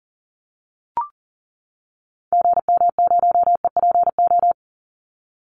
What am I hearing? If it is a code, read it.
Morse code, heard as a keyed tone, about 40 wpm, 700 Hz. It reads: GM0EPO
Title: Call Signs - 40wpm